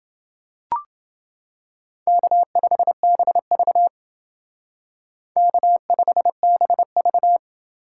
Morse code, heard as a keyed tone, about 29 wpm, 700 Hz. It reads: K5BV K5BV